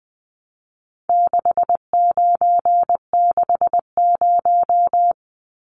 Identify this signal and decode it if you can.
Morse code, heard as a keyed tone, about 20 wpm, 700 Hz. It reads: 6960
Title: Numbers - 4-Digit 20wpm